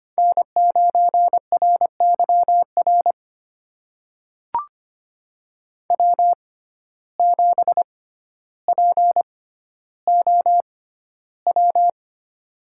Morse code, heard as a keyed tone, about 25 wpm, spaced as for 13 wpm, 700 Hz. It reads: N9RYR W7POW
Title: USA Call Signs - 5 Characters - 6x Character Spacing - 25wpm